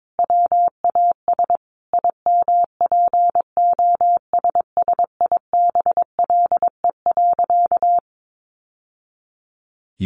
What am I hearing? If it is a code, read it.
Morse code, heard as a keyed tone, about 22 wpm, 700 Hz. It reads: WAS IMPOSSIBLE.
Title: Sentences from Top 1100 Words 22wpm